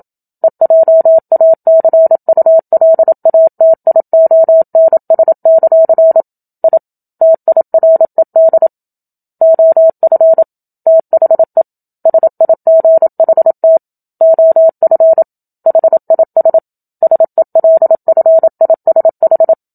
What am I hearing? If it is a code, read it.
Morse code, heard as a keyed tone, about 27 wpm, 650 Hz. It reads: EJACULATIONS; I TIRED OF THE SIGHT OF HIS SELFISH
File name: Book Two / 01-Chapter 1 - 27 WPM